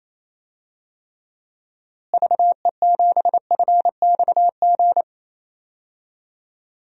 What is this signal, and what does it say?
Morse code, heard as a keyed tone, about 28 wpm, 700 Hz. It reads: VE7FXG
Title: DXCC-100 Call Signs - 6 Characters - Repeated 2x - 28wpm